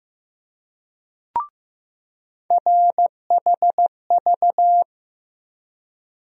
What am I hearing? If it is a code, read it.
Morse code, heard as a keyed tone, about 15 wpm, 700 Hz. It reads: RHV
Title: USA Call Sign Postfixes - 3 Letters 15wpm